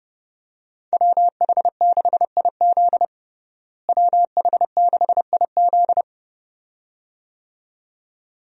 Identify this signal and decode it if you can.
Morse code, heard as a keyed tone, about 30 wpm, 700 Hz. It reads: WH6IZ WH6IZ
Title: USA Call Signs - 5 Characters - Repeated 2x - 30wpm